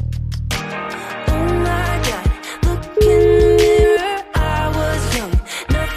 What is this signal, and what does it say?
Ringback tone (Europe, 425 Hz), with noise.